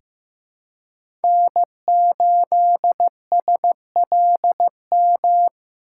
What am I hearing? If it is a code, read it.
Morse code, heard as a keyed tone, about 15 wpm, 700 Hz. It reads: N8SLM
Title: USA Call Signs - 3x Character Spacing - 15wpm